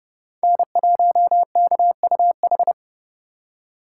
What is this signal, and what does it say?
Morse code, heard as a keyed tone, about 35 wpm, 700 Hz. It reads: N1KUH
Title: Speed Racing - USA 1x3 Call Signs 30wpm